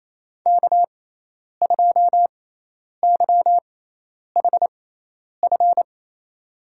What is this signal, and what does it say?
Morse code, heard as a keyed tone, about 28 wpm, spaced as for 10 wpm, 700 Hz. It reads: K2YHF
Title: USA Call Signs - 5 Characters - 6x Character Spacing - 28wpm